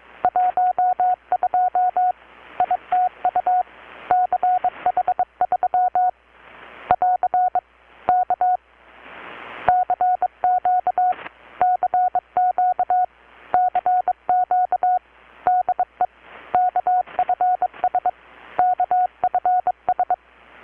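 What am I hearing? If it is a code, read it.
Morse code, heard as a keyed tone, about 22 wpm, 720 Hz. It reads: 12 ITU CH3 <AR> K CQ CQ CQ DE KFS KFS
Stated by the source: Morse from KPH's Night of Nights XX maritime radiotelegraphy event, received in Chiapas, Mexico about 3,800 km away on a Kenwood R-600 with a grounded 10 m long-wire antenna.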